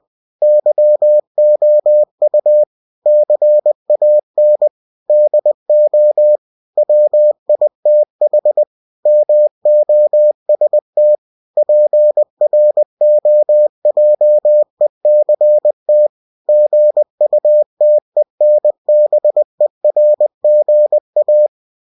Morse code, heard as a keyed tone, about 20 wpm, 600 Hz. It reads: YOU CAN DO WITH MOST PROJECT GUTENBERGA
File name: cw-books / winnie-the-pooh / winnie-20wpm-page-0113